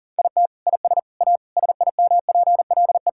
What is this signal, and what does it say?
Morse code, heard as a keyed tone, about 40 wpm, 700 Hz. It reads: IT IS A SIMPLE
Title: Sentences from Top 1000 Words - Review Part 6 - 40wpm